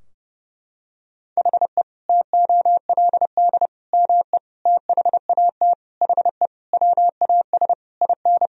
Morse code, heard as a keyed tone, about 30 wpm, 700 Hz. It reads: HE TOLD ME THAT HE WAS IN
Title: Sentences from Top 1000 Words - Review Part 3 - 30wpm